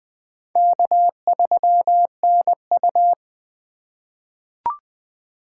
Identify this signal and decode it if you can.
Morse code, heard as a keyed tone, about 20 wpm, 700 Hz. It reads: K3NU
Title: USA Call Signs - 4 Characters - 4x Character Spacing - 20wpm